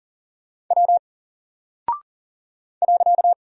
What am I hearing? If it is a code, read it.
Morse code, heard as a keyed tone, about 38 wpm, 700 Hz. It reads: W .
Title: Single Character (Letter:Number:Punctuation:Prosign) Alt 40wpm